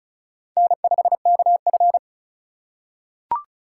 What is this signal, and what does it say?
Morse code, heard as a keyed tone, about 34 wpm, 700 Hz. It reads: N5KF
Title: USA Call Signs - 4 Characters - 4x Character Spacing - 35wpm